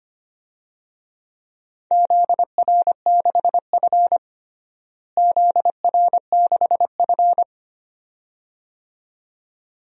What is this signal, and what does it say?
Morse code, heard as a keyed tone, about 25 wpm, 700 Hz. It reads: ZR6F ZR6F
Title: DXCC-100 Call Signs - 4 Characters - Repeated 2x - 25wpm